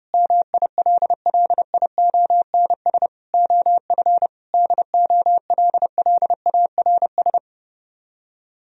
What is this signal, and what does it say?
Morse code, heard as a keyed tone, about 28 wpm, 700 Hz. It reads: MILLIONS OF DOLLARS
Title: Speed-Racing - Sets of 5 Words 30wpm